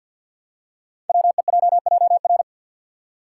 Morse code, heard as a keyed tone, about 50 wpm, 700 Hz. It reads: WEJJR
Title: Code Groups - 5 Letters - 50wpm